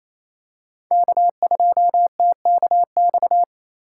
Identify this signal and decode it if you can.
Morse code, heard as a keyed tone, about 27 wpm, 700 Hz. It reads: K2TKX
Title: USA Call Signs - 5 Characters - 2x Character Spacing - 28wpm